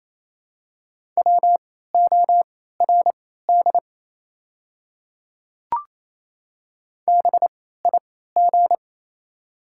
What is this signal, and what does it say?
Morse code, heard as a keyed tone, about 28 wpm, spaced as for 16 wpm, 700 Hz. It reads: WORD BIG
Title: 3x Character Spacing - Top 500 Words - 28wpm